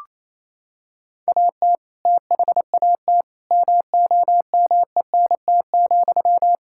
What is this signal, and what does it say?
Morse code, heard as a keyed tone, about 28 wpm, 700 Hz. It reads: AT THAT MOMENT,
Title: Sentences from Top 1600 Words 28wpm